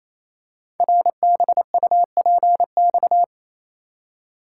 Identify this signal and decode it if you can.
Morse code, heard as a keyed tone, about 27 wpm, 690 Hz. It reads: RBUPX
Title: Code Groups - 5 Letters - 28wpm